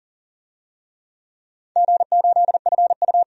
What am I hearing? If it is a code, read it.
Morse code, heard as a keyed tone, about 40 wpm, 700 Hz. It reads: G8FU
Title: DXCC-100 Call Signs - 4 Characters - 40wpm